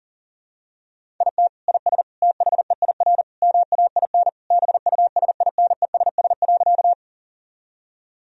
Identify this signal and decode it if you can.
Morse code, heard as a keyed tone, about 40 wpm, 700 Hz. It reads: IT IS THEIR MAIN BUSINESS.
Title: Sentences from Top 1000 Words - Review Part 2 - 40wpm